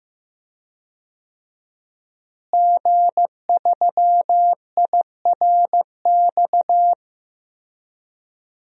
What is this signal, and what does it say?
Morse code, heard as a keyed tone, about 16 wpm, 700 Hz. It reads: G3IRX
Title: DXCC-100 Call Signs - 5 Characters - Repeated 3x - 15wpm